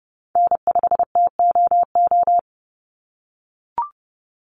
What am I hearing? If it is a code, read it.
Morse code, heard as a keyed tone, about 30 wpm, 700 Hz. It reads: N5TOO
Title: Call Signs - 30wpm